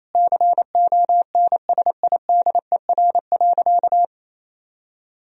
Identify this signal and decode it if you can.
Morse code, heard as a keyed tone, about 27 wpm, 700 Hz. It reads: CONSIDER.
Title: Sentences from Top 1000 Words - Review Part 4 - 28wpm